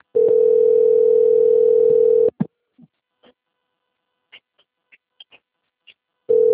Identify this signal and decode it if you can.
Ringback tone (North America), 440 + 480 Hz.
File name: out- - - - - .18452